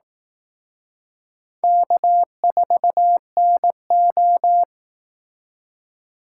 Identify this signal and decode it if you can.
Morse code, heard as a keyed tone, about 19 wpm, 700 Hz. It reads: K4NO
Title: Speed Racing - USA 1x2 Call Signs 15wpm